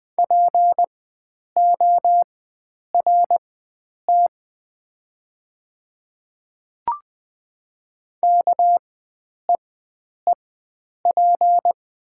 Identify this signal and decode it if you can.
Morse code, heard as a keyed tone, about 20 wpm, spaced as for 9 wpm, 700 Hz. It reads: PORT KEEP
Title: 4x Character Spacing - Top 500 Words - 20wpm